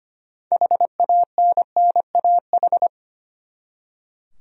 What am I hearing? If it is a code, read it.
Morse code, heard as a keyed tone, about 25 wpm, 700 Hz. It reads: HANNAH
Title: USA Common Names - 25wpm